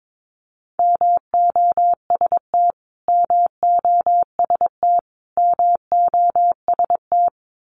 Morse code, heard as a keyed tone, about 22 wpm, 700 Hz. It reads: MOST MOST MOST
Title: Top 300 Words Repeated 22wpm